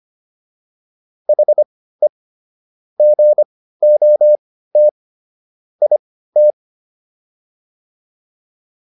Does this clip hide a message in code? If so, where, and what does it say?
Yes: Morse code from 1.3 to 6.5 s, heard as a keyed tone, about 25 wpm, spaced as for 15 wpm, 600 Hz. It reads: HE GOT IT